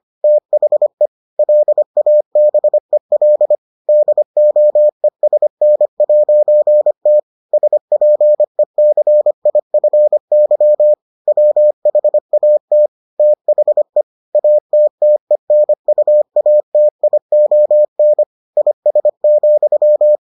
Morse code, heard as a keyed tone, about 25 wpm, 600 Hz. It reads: THE LABEL DOESN'T SPECIFY WHAT THE ATTENUATION IS,